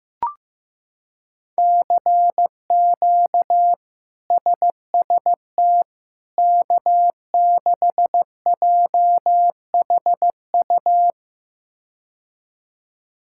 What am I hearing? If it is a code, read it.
Morse code, heard as a keyed tone, about 15 wpm, 700 Hz. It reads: CQ SST K6JHU
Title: K1USN SST QSO Practice - 15wpm